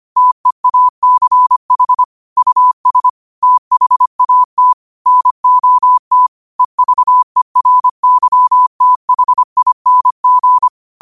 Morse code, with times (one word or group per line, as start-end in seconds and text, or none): 0.2-2.0 TEACH
2.4-3.1 US
3.4-4.7 THAT
5.1-6.3 NOT
6.6-10.7 EVERYTHING